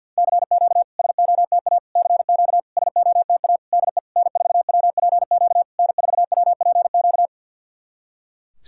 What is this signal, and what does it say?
Morse code, heard as a keyed tone, about 50 wpm, 700 Hz. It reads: CQ SOTA CQ SOTA DE N4WPQ N4WPQ
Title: SOTA QSO Practice 50wpm